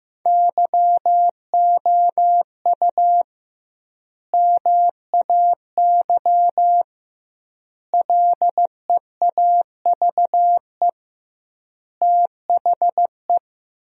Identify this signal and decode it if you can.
Morse code, heard as a keyed tone, about 15 wpm, 700 Hz. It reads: YOU MAY LEAVE THE
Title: 2x Word-Spacing - Sentences from Top 400 Words - 15wpm